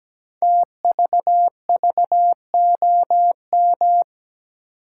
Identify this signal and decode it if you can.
Morse code, heard as a keyed tone, about 17 wpm, 700 Hz. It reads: TVVOM
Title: Code Groups - 5 Letters - 17wpm